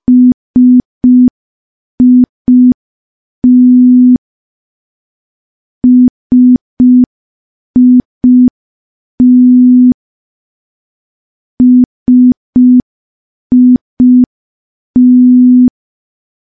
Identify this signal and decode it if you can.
Morse code, heard as a keyed tone, about 5 wpm, 250 Hz. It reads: SIT SIT SIT
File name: morse / morse-training-05WPM